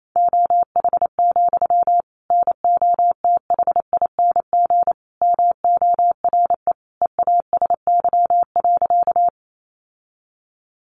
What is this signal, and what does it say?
Morse code, heard as a keyed tone, about 28 wpm, 700 Hz. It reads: OH, NOTHING MORE EASY.
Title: Sentences from Top 700 Words 28wpm